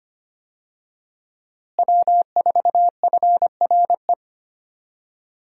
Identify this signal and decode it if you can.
Morse code, heard as a keyed tone, about 25 wpm, 700 Hz. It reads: W4FRE
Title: USA Call Signs - 5 Characters - Repeated 3x - 25wpm